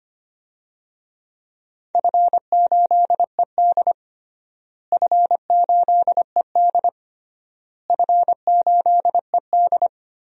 Morse code, heard as a keyed tone, about 25 wpm, 700 Hz. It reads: F8ED F8ED F8ED
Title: DXCC-100 Call Signs - 4 Characters - Repeated 3x - 25wpm